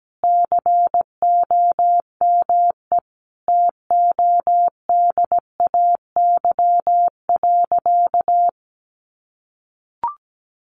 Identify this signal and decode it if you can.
Morse code, heard as a keyed tone, about 17 wpm, 700 Hz. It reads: COME TODAY.
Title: Easy Phrases 17wpm